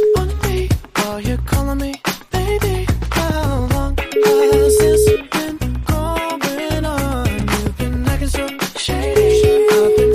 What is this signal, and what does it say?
Ringback tone (Europe, 425 Hz), with noise.